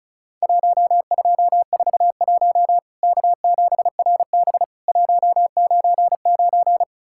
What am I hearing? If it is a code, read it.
Morse code, heard as a keyed tone, about 36 wpm, 700 Hz. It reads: 1241 K7RB 199
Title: Worked All Europe DX Contest - QTC Exchange - 35wpm